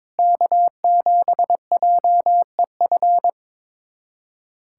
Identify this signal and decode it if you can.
Morse code, heard as a keyed tone, about 21 wpm, 700 Hz. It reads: K7JEF
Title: USA Callsign DE Callsign - 5 Characters - 22wpm